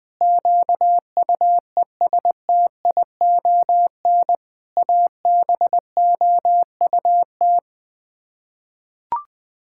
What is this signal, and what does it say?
Morse code, heard as a keyed tone, about 19 wpm, 700 Hz. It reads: QUESTION ABOUT
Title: Sets of 6 Words - 20wpm